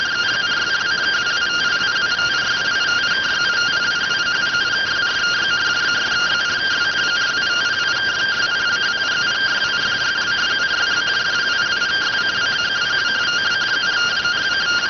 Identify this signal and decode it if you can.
Two-tone teleprinter signal, about 170 Hz shift.